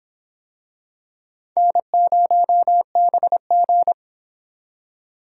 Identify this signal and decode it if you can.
Morse code, heard as a keyed tone, about 23 wpm, 700 Hz. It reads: N0BG